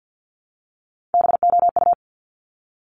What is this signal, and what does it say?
Morse code, heard as a keyed tone, about 50 wpm, 700 Hz. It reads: BOU
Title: Top Three Letter Combinations in Words 50wpm